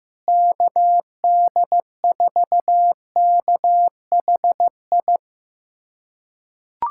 Morse code, heard as a keyed tone, about 16 wpm, 700 Hz. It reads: KD4KHI
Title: USA Callsign DE Callsign - 6 Characters - 15wpm